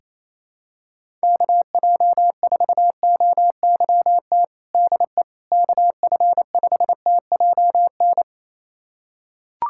Morse code, heard as a keyed tone, about 28 wpm, 700 Hz. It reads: KJ4OYT DE KF5TJN
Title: USA Callsign DE Callsign - 6 Characters - 28wpm